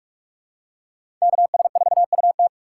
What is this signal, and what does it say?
Morse code, heard as a keyed tone, about 44 wpm, 700 Hz. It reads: KS4UT
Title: USA Call Signs - 5 Characters - 2x Character Spacing - 45wpm